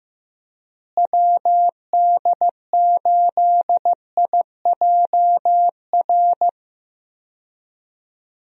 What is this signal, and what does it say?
Morse code, heard as a keyed tone, about 15 wpm, 700 Hz. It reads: WD8IJR
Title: USA Call Signs - 6 Characters - Repeated 3x - 15wpm